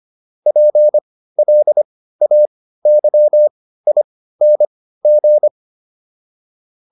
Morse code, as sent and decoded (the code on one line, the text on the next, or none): .--. .-.. .- -.-- .. -. --.
PLAYING